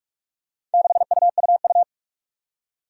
Morse code, heard as a keyed tone, about 44 wpm, 700 Hz. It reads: 6UUU
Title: Four Challenging Letter-Number Combinations 45wpm